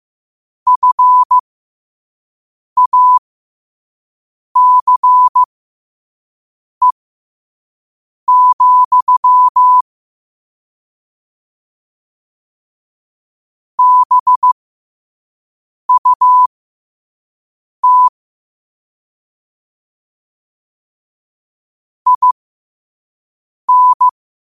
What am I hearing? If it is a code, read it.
Morse code, heard as a keyed tone, about 15 wpm, spaced as for 5 wpm, 1000 Hz. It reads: FACE, BUT IN
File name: qotdfiles / 5wpm